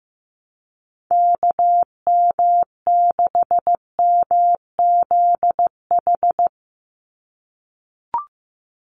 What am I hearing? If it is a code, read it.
Morse code, heard as a keyed tone, about 15 wpm, 700 Hz. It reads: KM6MZH
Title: USA Call Signs 15wpm